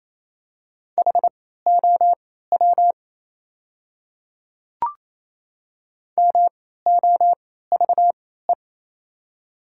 Morse code, heard as a keyed tone, about 28 wpm, spaced as for 16 wpm, 700 Hz. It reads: HOW MOVE
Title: 3x Character Spacing - Top 200 Words - 28wpm